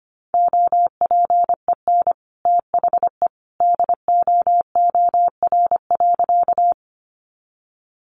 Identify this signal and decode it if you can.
Morse code, heard as a keyed tone, about 24 wpm, 700 Hz. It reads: OPEN THE DOOR.
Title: Sentences from Top 300 Words 25wpm